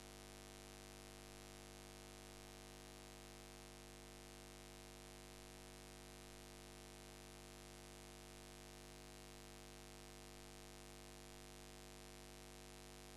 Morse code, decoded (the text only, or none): none